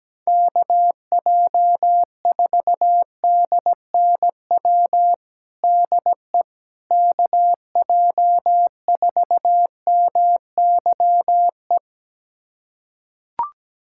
Morse code, heard as a keyed tone, about 17 wpm, 700 Hz. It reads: KJ4DNW DE KJ4MYE
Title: USA Callsign DE Callsign - 6 Characters - 17wpm